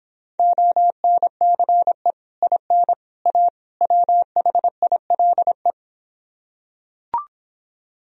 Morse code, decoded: ONCE IN A WHILE

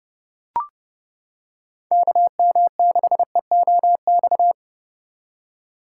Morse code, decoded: KM6EOX

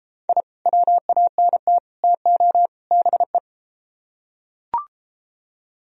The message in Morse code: I WANT TO BE

